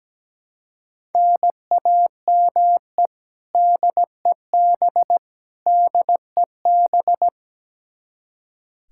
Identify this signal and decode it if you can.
Morse code, heard as a keyed tone, about 17 wpm, 700 Hz. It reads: NAME DEB DEB
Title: Practice QSO - Exchange 1 - Beginning - 17wpm